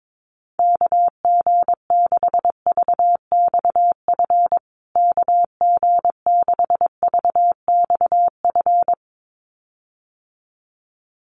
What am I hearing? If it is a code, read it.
Morse code, heard as a keyed tone, about 22 wpm, 700 Hz. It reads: KG6VXF KG6VXF
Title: USA Call Signs Repeated 2x 22wpm